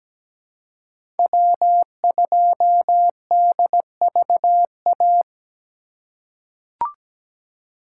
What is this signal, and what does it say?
Morse code, heard as a keyed tone, about 17 wpm, 700 Hz. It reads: W2DVA
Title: USA Call Signs - 5 Characters - 3x Character Spacing - 17wpm